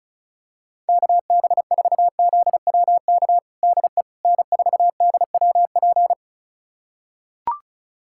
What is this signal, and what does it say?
Morse code, heard as a keyed tone, about 34 wpm, 700 Hz. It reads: KB4ZWK DE N4DWP
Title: USA Callsign DE Callsign 35wpm